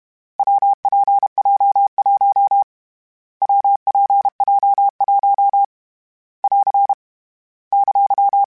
Morse code, heard as a keyed tone, about 32 wpm, 800 Hz. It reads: WPJ1 WPJ1 <AR> !
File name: Difficult Combos / Difficult Combos-mp3s / WPJ WPJ1 @30